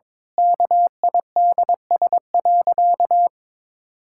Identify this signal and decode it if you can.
Morse code, heard as a keyed tone, about 22 wpm, 700 Hz. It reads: KIDS.
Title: Pangram Challenge v2 - 22wpm